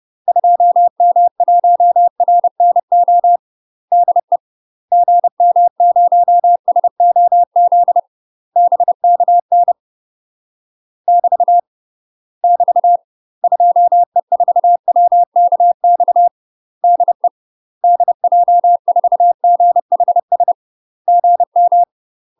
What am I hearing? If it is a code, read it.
Morse code, heard as a keyed tone, about 30 wpm, 700 Hz. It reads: 2M1RNO DE GM0SOZ BKN <BT> <BT> 2E4WKX DE DJ4GHS GM
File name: UK MASTER QSOs TRAINING / 30WPM / UK MASTER QSOs 0038